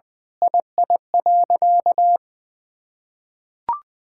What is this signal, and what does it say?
Morse code, heard as a keyed tone, about 21 wpm, 700 Hz. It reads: II.
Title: Sentences with 7-Word Phrases - 20wpm